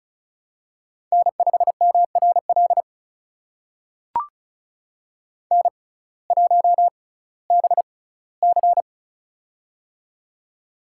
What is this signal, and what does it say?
Morse code, heard as a keyed tone, about 36 wpm, spaced as for 19 wpm, 700 Hz. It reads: N5MRL N1BC
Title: USA Call Signs - 6x Character Spacing - 35wpm